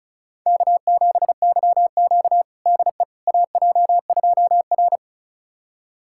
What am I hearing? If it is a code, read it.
Morse code, heard as a keyed tone, about 36 wpm, 700 Hz. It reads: K7YQ DE AJ2R
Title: USA Callsign DE Callsign - 4 Characters - 35wpm